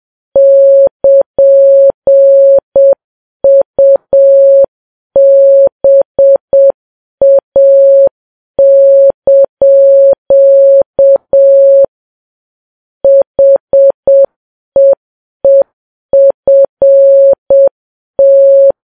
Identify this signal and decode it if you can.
Morse code, heard as a keyed tone, about 7 wpm, 550 Hz. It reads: <KN>UBA) HEEFT
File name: Files Web tls / Morse on9fv / morse7kps